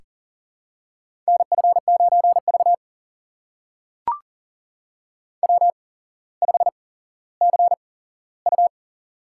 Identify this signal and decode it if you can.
Morse code, heard as a keyed tone, about 40 wpm, spaced as for 17 wpm, 700 Hz. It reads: NF9V W5CU
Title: USA Call Signs - 4 Characters - 8x Character Spacing - 40wpm